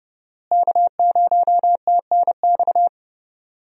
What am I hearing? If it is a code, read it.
Morse code, heard as a keyed tone, about 30 wpm, 700 Hz. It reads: K0TNX
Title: USA Call Signs - 5 Characters - Repeated 3x - 30wpm